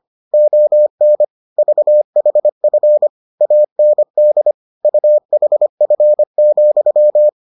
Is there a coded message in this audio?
Yes: Morse code from 0.3 to 7.3 s, heard as a keyed tone, about 25 wpm, 600 Hz. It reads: ON VHF AND UHF,